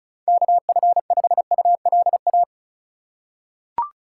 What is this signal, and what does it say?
Morse code, heard as a keyed tone, about 36 wpm, 700 Hz. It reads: KF5ULA
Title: USA Call Signs - 4x Character Spacing - 35wpm